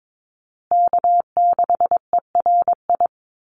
Morse code, heard as a keyed tone, about 22 wpm, 700 Hz. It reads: K6ERI
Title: USA Call Signs Repeated 3x 22wpm